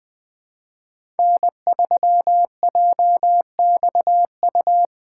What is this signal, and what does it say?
Morse code, heard as a keyed tone, about 21 wpm, 700 Hz. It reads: N3JXU
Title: USA Call Signs - 3x Character Spacing - 20wpm